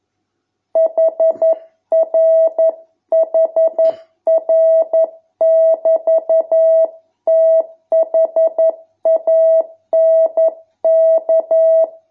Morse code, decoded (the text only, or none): HRHR<BT>THANK